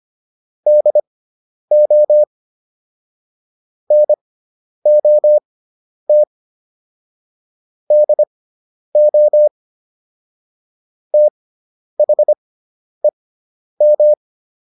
Morse code, heard as a keyed tone, about 25 wpm, spaced as for 10 wpm, 600 Hz. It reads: DO NOT DO THEM